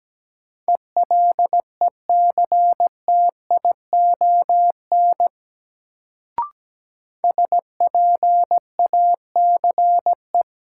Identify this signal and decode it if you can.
Morse code, heard as a keyed tone, about 17 wpm, 700 Hz. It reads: ELECTION SPACE
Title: Sentences from Top 1500 Words - New Word Drill Alt - 17wpm